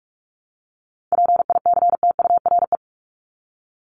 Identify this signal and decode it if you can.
Morse code, heard as a keyed tone, about 44 wpm, 700 Hz. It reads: PICTURE
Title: Top 500 Words Repeated 45wpm